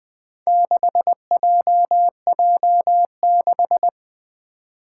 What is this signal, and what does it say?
Morse code, heard as a keyed tone, about 20 wpm, 700 Hz. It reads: 6JJ6
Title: Four Challenging Letter-Number Combinations 20wpm